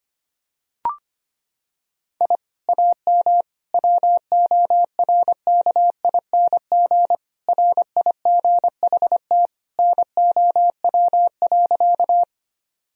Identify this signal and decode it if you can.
Morse code, heard as a keyed tone, about 25 wpm, 700 Hz. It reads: I AM WORKING RIGHT NOW.
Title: Sentences from Top 1000 Words - Review Part 4 - 25wpm